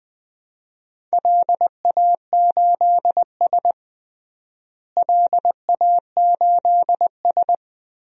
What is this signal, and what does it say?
Morse code, heard as a keyed tone, about 20 wpm, 700 Hz. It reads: LA8S LA8S
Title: DXCC-100 Call Signs - 4 Characters - Repeated 2x - 20wpm